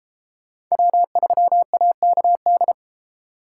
Morse code, heard as a keyed tone, about 30 wpm, 700 Hz. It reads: W3AKD